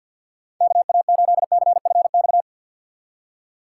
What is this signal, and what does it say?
Morse code, heard as a keyed tone, about 63 wpm, 700 Hz. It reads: KA8CFX